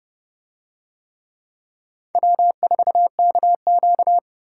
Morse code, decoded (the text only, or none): W4KQ